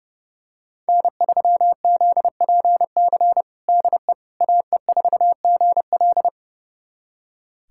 Morse code, heard as a keyed tone, about 30 wpm, 700 Hz. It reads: N3ZPC DE AE4GL
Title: USA Callsign DE Callsign - 5 Characters - 30wpm